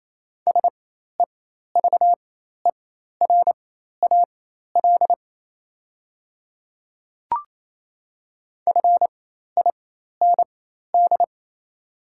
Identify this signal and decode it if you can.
Morse code, heard as a keyed tone, about 28 wpm, spaced as for 13 wpm, 690 Hz. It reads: SEVERAL FIND
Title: 4x Character Spacing - Top 500 Words - 28wpm